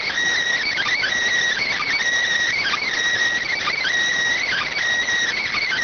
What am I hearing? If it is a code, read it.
Emergency Alert System (SAME) tones, heard as harsh buzzing bursts. It is heard from 0.7 to 4.7 s.